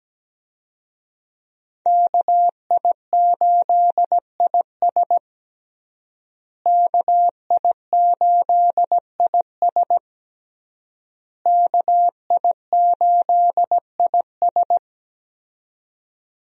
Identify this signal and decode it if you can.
Morse code, heard as a keyed tone, about 17 wpm, 700 Hz. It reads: KI8IS KI8IS KI8IS
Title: USA Call Signs - 5 Characters - Repeated 3x - 17wpm